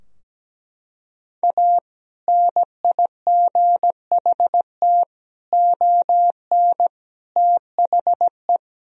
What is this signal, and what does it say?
Morse code, heard as a keyed tone, about 17 wpm, 700 Hz. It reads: A NIGHT ON THE
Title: Sentences from Top 1000 Words - Review Part 3 - 17wpm